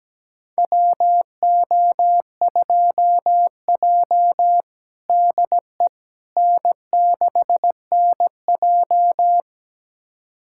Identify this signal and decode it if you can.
Morse code, heard as a keyed tone, about 17 wpm, 700 Hz. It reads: WO2J DE N6NJ
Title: USA Callsign DE Callsign - 4 Characters - 17wpm